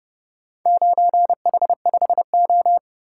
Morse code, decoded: 9H5O

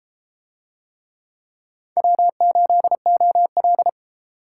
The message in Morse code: W8OL

